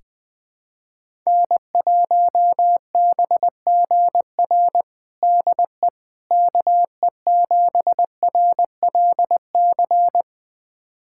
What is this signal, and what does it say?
Morse code, heard as a keyed tone, about 21 wpm, 700 Hz. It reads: N1BGR DE KE7RLC
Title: USA Callsign DE Callsign 20wpm